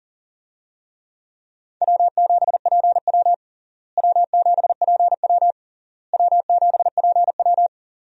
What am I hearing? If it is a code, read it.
Morse code, heard as a keyed tone, about 37 wpm, 700 Hz. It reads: W7PW W7PW W7PW
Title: USA Call Signs - 4 Characters - Repeated 3x - 40wpm